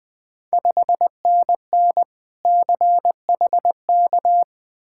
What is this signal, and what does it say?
Morse code, heard as a keyed tone, about 20 wpm, 690 Hz. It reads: 5NN CHK
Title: Ontario QSO Party Exchange 20wpm